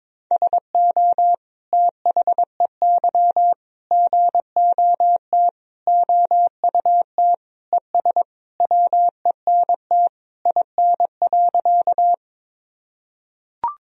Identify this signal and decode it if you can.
Morse code, heard as a keyed tone, about 22 wpm, 700 Hz. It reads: SO THEY GOT OUT ES WENT IN.
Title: Sentences from Top 300 Words and Abbreviations 22wpm